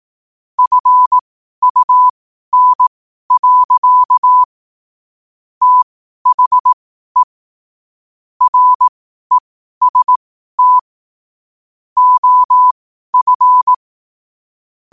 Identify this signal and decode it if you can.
Morse code, heard as a keyed tone, about 18 wpm, spaced as for 12 wpm, 1000 Hz. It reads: FUN. THE REST OF
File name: qotdfiles / 12wpm